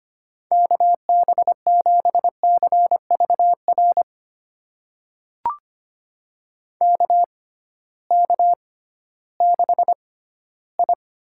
Morse code, heard as a keyed tone, about 25 wpm, spaced as for 14 wpm, 700 Hz. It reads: KB7CVR KK6I
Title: USA Call Signs - 6 Characters - 6x Character Spacing - 25wpm